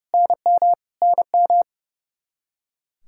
Morse code, heard as a keyed tone, about 30 wpm, 700 Hz. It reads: NM NM
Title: Practice QSO - Exchange 1 - Beginning - 30wpm